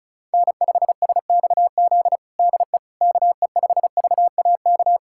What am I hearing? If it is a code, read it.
Morse code, heard as a keyed tone, about 36 wpm, 700 Hz. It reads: N5SXZ DE KE5VAK
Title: USA Callsign DE Callsign 35wpm